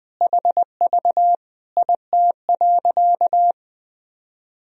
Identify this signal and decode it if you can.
Morse code, heard as a keyed tone, about 20 wpm, 700 Hz. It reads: HV IT.
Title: Sentences from Top 100 Words and Abbreviations 20wpm